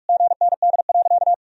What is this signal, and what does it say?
Morse code, heard as a keyed tone, about 46 wpm, 700 Hz. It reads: GND.
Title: Sentences from Top 500 Words and Abbreviations 45wpm